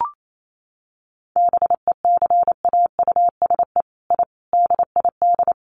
Morse code, heard as a keyed tone, about 28 wpm, 700 Hz. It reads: BECAUSE I DID
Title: Sets of 5 Words Encore2 28wpm